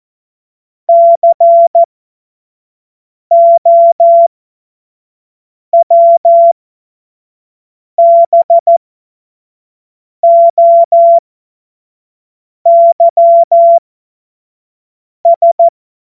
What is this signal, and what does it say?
Morse code, heard as a keyed tone, about 14 wpm, spaced as for 5 wpm, 680 Hz. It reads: COWBOYS
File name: Morsekurs / Witze / Witz 08. Februar 2022 25 F70